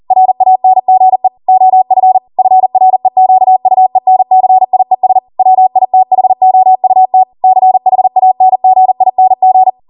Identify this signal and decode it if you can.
Morse code, heard as a keyed tone, about 40 wpm, 750 Hz. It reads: RANGE OF FREQUENCIES WITHOUT CHANGING